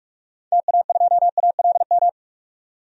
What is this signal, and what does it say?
Morse code, heard as a keyed tone, about 46 wpm, 700 Hz. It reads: TA2ALM
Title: DXCC-100 Call Signs - 6 Characters - 45wpm